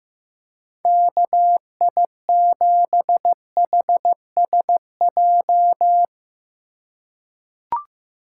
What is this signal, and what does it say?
Morse code, heard as a keyed tone, about 15 wpm, 700 Hz. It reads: KI7HSJ
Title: USA Call Signs - 6 Characters - 3x Character Spacing - 15wpm